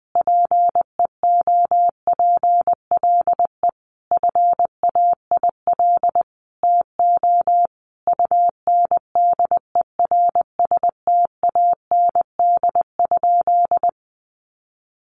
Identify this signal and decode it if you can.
Morse code, heard as a keyed tone, about 20 wpm, 700 Hz. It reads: PEOPLE FAIL TO UNDERSTAND?
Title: Sentences from Top 1100 Words 20wpm